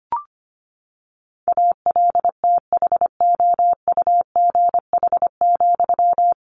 Morse code, heard as a keyed tone, about 25 wpm, 700 Hz. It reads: ALTHOUGH,
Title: Sentences from Top 600 Words 25wpm